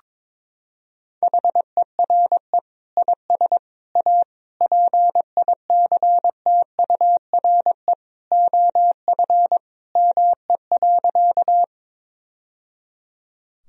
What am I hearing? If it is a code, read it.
Morse code, heard as a keyed tone, about 21 wpm, 700 Hz. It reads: HERE IS A PICTURE OF ME.
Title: Sentences from Top 1000 Words - Review Part 1 - 22wpm